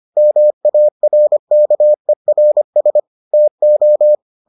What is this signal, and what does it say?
Morse code, heard as a keyed tone, about 25 wpm, 600 Hz. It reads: MARKERS TO